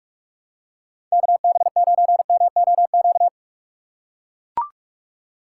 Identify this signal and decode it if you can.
Morse code, heard as a keyed tone, about 42 wpm, 700 Hz. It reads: KB9MOQ
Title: USA Call Signs - 6 Characters - 3x Character Spacing - 45wpm